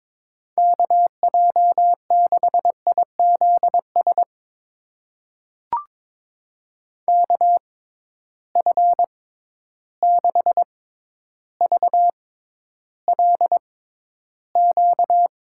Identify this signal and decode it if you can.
Morse code, heard as a keyed tone, about 22 wpm, spaced as for 11 wpm, 700 Hz. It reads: KJ6IZS KF6VLQ
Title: USA Call Signs - 6 Characters - 6x Character Spacing - 22wpm